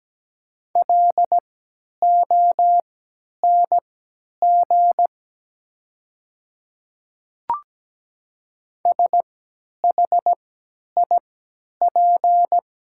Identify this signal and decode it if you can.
Morse code, heard as a keyed tone, about 17 wpm, spaced as for 10 wpm, 700 Hz. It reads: LONG SHIP
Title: 3x Character Spacing - Top 500 Words - 17wpm